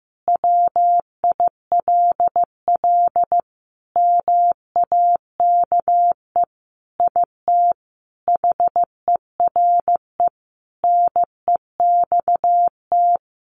Morse code, heard as a keyed tone, about 15 wpm, 700 Hz. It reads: WILL MAKE IT HERE NEXT
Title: Sentences from Top 1500 Words 15wpm